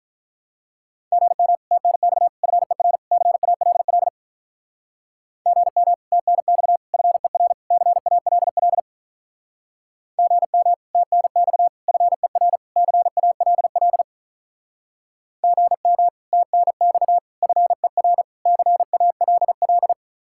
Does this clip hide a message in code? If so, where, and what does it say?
Yes: Morse code from 1.1 to 19.9 s, heard as a keyed tone, about 46 wpm, 700 Hz. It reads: GM TNX FER CALL GM TNX FER CALL GM TNX FER CALL GM TNX FER CALL